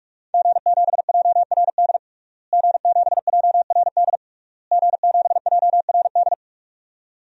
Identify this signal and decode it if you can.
Morse code, heard as a keyed tone, about 41 wpm, 700 Hz. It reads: G7JRD G7JRD G7JRD
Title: DXCC-100 Call Signs - 5 Characters - Repeated 3x - 45wpm